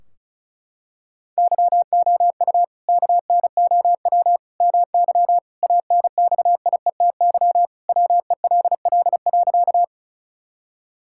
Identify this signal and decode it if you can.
Morse code, heard as a keyed tone, about 36 wpm, 700 Hz. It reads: YOU KNOW MY ANXIETY WELL.
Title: Sentences from Top 1600 Words 35wpm